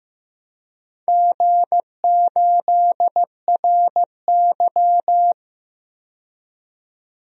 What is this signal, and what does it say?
Morse code, heard as a keyed tone, about 15 wpm, 700 Hz. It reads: G8RY
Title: DXCC-100 Call Signs - 4 Characters - 15wpm